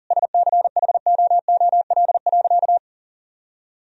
Morse code, heard as a keyed tone, about 37 wpm, 700 Hz. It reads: SCHOOL.
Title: Sentences from Top 1000 Words - Review Part 1 - 40wpm